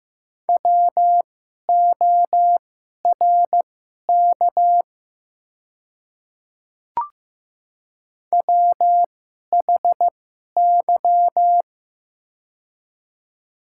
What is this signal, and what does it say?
Morse code, heard as a keyed tone, about 15 wpm, spaced as for 11 wpm, 700 Hz. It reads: WORK WHY
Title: 2x Character Spacing - Top 500 Words - 15wpm